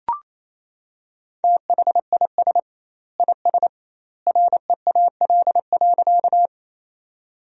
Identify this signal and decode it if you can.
Morse code, heard as a keyed tone, about 28 wpm, 700 Hz. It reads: THIS IS REAL.
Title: 2x Word-Spacing - Sentences from Top 300 Words - 28wpm